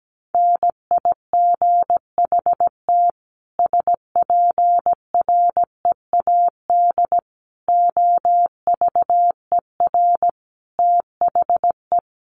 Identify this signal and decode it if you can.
Morse code, heard as a keyed tone, about 17 wpm, 700 Hz. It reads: NIGHT SPREAD OVER THE
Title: Sentences from Top 1200 Words 17wpm